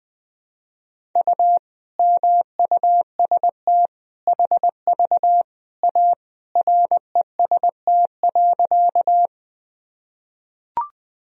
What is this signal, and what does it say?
Morse code, heard as a keyed tone, about 20 wpm, 700 Hz. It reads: U MUST HV A REST.
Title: Sentences from Top 500 Words and Abbreviations 20wpm